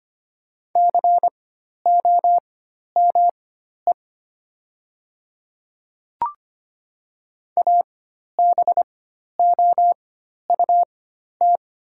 Morse code, heard as a keyed tone, about 25 wpm, spaced as for 12 wpm, 700 Hz. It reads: COME ABOUT
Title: 4x Character Spacing - Top 100 Words - 25wpm